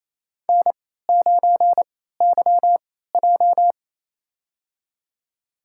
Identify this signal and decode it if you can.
Morse code, heard as a keyed tone, about 28 wpm, spaced as for 16 wpm, 700 Hz. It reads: N9YJ
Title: USA Call Signs - 4 Characters - 3x Character Spacing - 28wpm